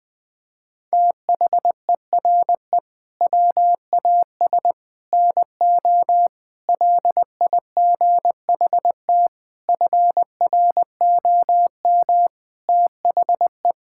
Morse code, heard as a keyed tone, about 21 wpm, 700 Hz. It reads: THERE WAS NO LIGHT FROM THE
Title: Sentences from Top 1000 Words - Review Part 5 - 20wpm